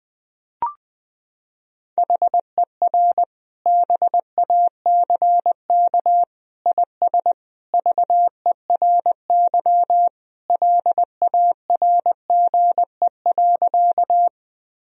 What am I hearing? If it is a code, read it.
Morse code, heard as a keyed tone, about 20 wpm, 700 Hz. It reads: HER BACK IS VERY LARGE.
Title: Sentences from Top 1000 Words - Review Part 1 - 20wpm